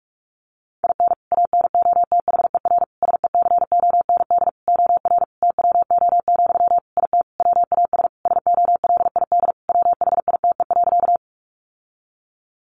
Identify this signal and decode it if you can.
Morse code, heard as a keyed tone, about 44 wpm, 700 Hz. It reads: IN ANOTHER SECOND OR TWO, IT WAS SOLID WHITE.
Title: Sentences from Top 1500 Words 45wpm